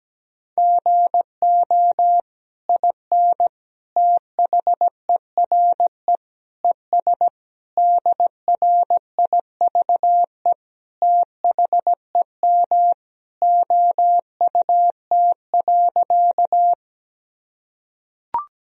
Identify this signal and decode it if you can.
Morse code, heard as a keyed tone, about 17 wpm, 700 Hz. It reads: GO IN THERE ES DRIVE THEM OUT.
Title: Sentences from Top 500 Words and Abbreviations 17wpm